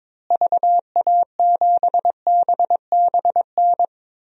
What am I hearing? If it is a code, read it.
Morse code, heard as a keyed tone, about 22 wpm, 700 Hz. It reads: VA7BBN
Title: DXCC-100 Call Signs - 6 Characters - Repeated 3x - 22wpm